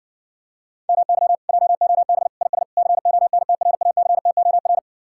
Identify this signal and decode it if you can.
Morse code, heard as a keyed tone, about 60 wpm, 700 Hz. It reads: MY JOB IS CONTRACTOR